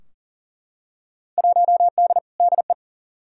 Morse code, heard as a keyed tone, about 40 wpm, 700 Hz. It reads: 1D DE
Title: Field Day Exchanges - 40wpm